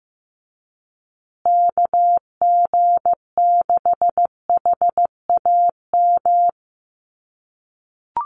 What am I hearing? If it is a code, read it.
Morse code, heard as a keyed tone, about 15 wpm, 700 Hz. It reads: KG6HAM
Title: USA Call Signs Repeated 2x 15wpm